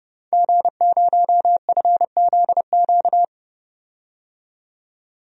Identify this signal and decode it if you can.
Morse code, heard as a keyed tone, about 30 wpm, 690 Hz. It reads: G0FZQ
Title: DXCC-100 Call Signs - 5 Characters - 30wpm